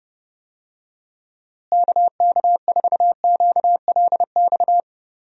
Morse code, heard as a keyed tone, about 30 wpm, 700 Hz. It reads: KK4QLX
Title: USA Call Signs - 6 Characters - Repeated 3x - 30wpm